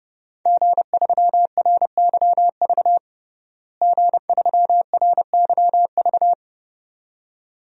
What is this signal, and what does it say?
Morse code, heard as a keyed tone, about 30 wpm, 700 Hz. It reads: G3RYV G3RYV
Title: DXCC-100 Call Signs - 5 Characters - Repeated 2x - 30wpm